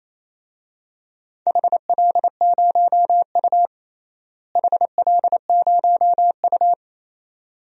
Morse code, heard as a keyed tone, about 28 wpm, 700 Hz. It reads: HL0U HL0U
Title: DXCC-100 Call Signs - 4 Characters - Repeated 2x - 28wpm